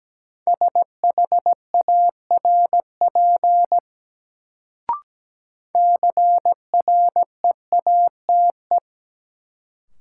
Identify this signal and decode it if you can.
Morse code, heard as a keyed tone, about 17 wpm, 700 Hz. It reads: SHARP CREATE
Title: Sentences from Top 1600 Words - New Word Drill Alt - 17wpm